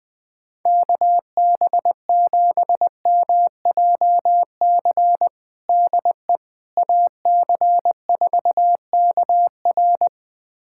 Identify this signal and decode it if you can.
Morse code, heard as a keyed tone, about 21 wpm, 700 Hz. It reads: KB7MJC DE AC4KR
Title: USA Callsign DE Callsign 20wpm